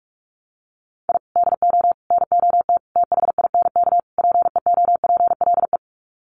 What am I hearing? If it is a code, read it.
Morse code, heard as a keyed tone, about 44 wpm, 700 Hz. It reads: I DO NOT THINK PEOPLE
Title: Sets of 5 Words Encore2 45wpm